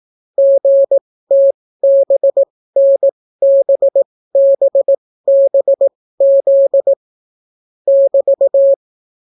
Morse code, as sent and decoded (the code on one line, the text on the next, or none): --. - -... -. -... -... -... --.. / -...-
GTBNBBBZ <BT>